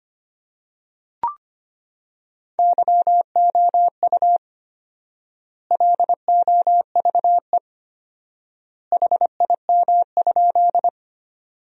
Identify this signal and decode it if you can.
Morse code, heard as a keyed tone, about 25 wpm, 700 Hz. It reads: YOU LOVE HIM?
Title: 4x Word-Spacing - Sentences from Top 400 Words - 25wpm